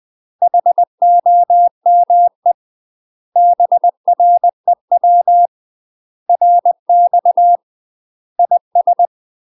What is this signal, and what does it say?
Morse code, heard as a keyed tone, about 20 wpm, 700 Hz. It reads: HOME BREW RX IS
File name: UK MASTER QSOs TRAINING / 20WPM / UK MASTER QSOs 0041